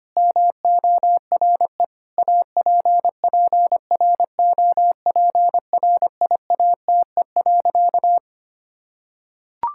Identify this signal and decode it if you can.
Morse code, heard as a keyed tone, about 25 wpm, 700 Hz. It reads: MORE APPROPRIATE.
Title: Sentences from Top 1600 Words 25wpm